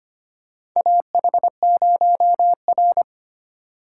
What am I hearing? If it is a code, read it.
Morse code, heard as a keyed tone, about 25 wpm, 700 Hz. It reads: AH0R
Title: USA Call Signs - 4 Characters - 6x Character Spacing - 25wpm